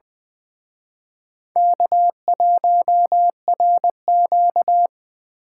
Morse code, heard as a keyed tone, about 23 wpm, 700 Hz. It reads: K1RQ
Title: Speed Racing - USA Call Signs 20wpm